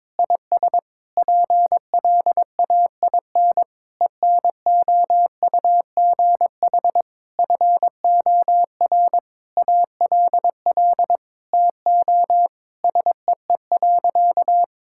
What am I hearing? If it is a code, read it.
Morse code, heard as a keyed tone, about 22 wpm, 700 Hz. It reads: IS PLAIN ENOUGH FOR ALL TO SEE.
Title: Sentences from Top 1000 Words - Review Part 5 - 22wpm